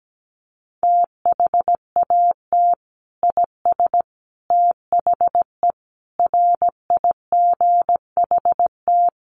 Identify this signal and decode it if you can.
Morse code, heard as a keyed tone, about 17 wpm, 700 Hz. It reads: THAT IS THE RIGHT
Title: Sentences from Top 1400 Words 17wpm